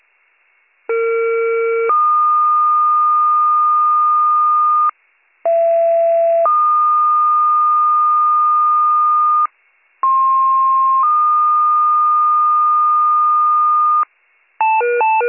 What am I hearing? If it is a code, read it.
Two-tone page, heard as a one-second tone, then a three-second tone, about 460 then 1220 Hz, about 670 then 1220 Hz, about 1060 then 1220 Hz. It is heard from 0.9 to 14.0 s.